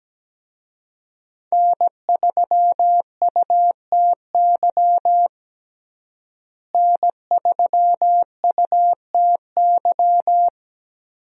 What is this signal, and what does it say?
Morse code, heard as a keyed tone, about 17 wpm, 700 Hz. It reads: N3UTY N3UTY
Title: USA Call Signs - 5 Characters - Repeated 2x - 17wpm